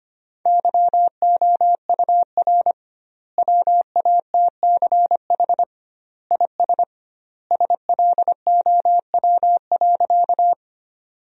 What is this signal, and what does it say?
Morse code, heard as a keyed tone, about 25 wpm, 700 Hz. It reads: YOUR WATCH IS SLOW.
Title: 2x Word-Spacing - Sentences from Top 400 Words - 25wpm